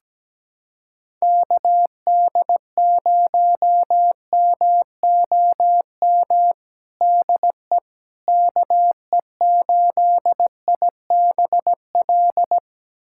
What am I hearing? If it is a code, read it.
Morse code, heard as a keyed tone, about 17 wpm, 700 Hz. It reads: KD0MOM DE KE8IBL
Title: USA Callsign DE Callsign - 6 Characters - 17wpm